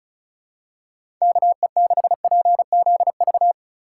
Morse code, heard as a keyed tone, about 34 wpm, 700 Hz. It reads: KE6PZV
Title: USA Call Signs - 6 Characters - 6x Character Spacing - 35wpm